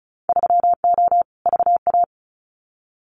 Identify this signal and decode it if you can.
Morse code, heard as a keyed tone, about 34 wpm, 700 Hz. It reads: 3O VA
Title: Winter Field Day Exchanges 35wpm